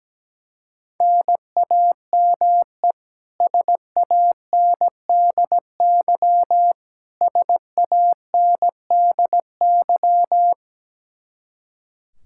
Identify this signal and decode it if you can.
Morse code, heard as a keyed tone, about 17 wpm, 700 Hz. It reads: NAME SANDY SANDY
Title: K1USN SST QSO Practice 17wpm